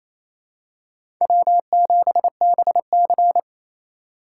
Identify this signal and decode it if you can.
Morse code, heard as a keyed tone, about 27 wpm, 700 Hz. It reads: W7BC
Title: USA Call Signs - 4x Character Spacing - 28wpm